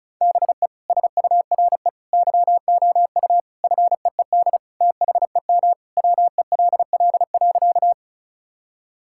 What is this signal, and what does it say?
Morse code, heard as a keyed tone, about 34 wpm, 700 Hz. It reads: BE SURE YOU FEED THEM WELL.
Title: Sentences from Top 1600 Words 35wpm